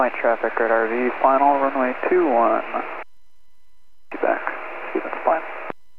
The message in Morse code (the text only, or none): none